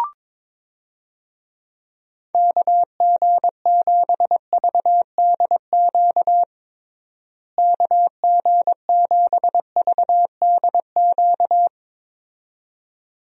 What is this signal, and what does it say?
Morse code, heard as a keyed tone, about 21 wpm, 700 Hz. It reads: KG7VDQ KG7VDQ
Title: USA Call Signs - 6 Characters - Repeated 2x - 22wpm